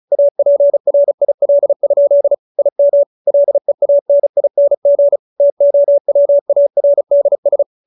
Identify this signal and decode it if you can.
Morse code, heard as a keyed tone, about 34 wpm, 570 Hz. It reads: APRIL? IM LEANING TOWARDS
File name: QSO312 35